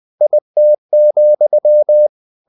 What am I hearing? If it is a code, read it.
Morse code, heard as a keyed tone, about 20 wpm, 600 Hz. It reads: IT,
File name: the old man and the sea part 2 20wpm